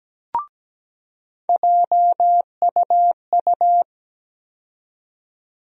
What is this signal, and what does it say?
Morse code, heard as a keyed tone, about 17 wpm, 700 Hz. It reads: JUU
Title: USA Call Sign Postfixes - 3 Letters 17wpm